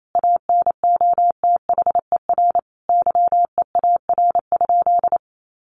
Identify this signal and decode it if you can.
Morse code, heard as a keyed tone, about 28 wpm, 700 Hz. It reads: ANOTHER YEAR?
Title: Sentences from Top 1500 Words 28wpm